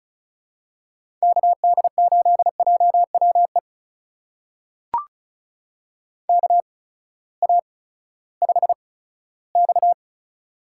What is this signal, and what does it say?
Morse code, heard as a keyed tone, about 36 wpm, spaced as for 18 wpm, 700 Hz. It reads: KD8JWE KA5X
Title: USA Call Signs - 8x Character Spacing - 35wpm